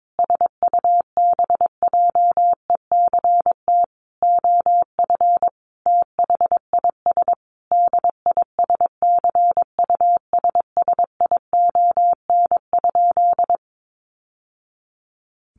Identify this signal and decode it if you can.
Morse code, heard as a keyed tone, about 22 wpm, 700 Hz. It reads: SUBJECT OF THIS DISCUSSION?
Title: Sentences from Top 1100 Words 22wpm